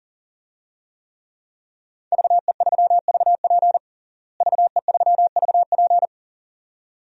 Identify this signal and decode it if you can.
Morse code, heard as a keyed tone, about 37 wpm, 700 Hz. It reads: VE3VP VE3VP
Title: DXCC-100 Call Signs - 5 Characters - Repeated 2x - 40wpm